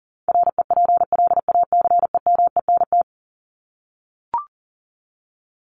Morse code, heard as a keyed tone, about 40 wpm, 700 Hz. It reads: REPLACEMENT
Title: Top 1751-2000 Words 40wpm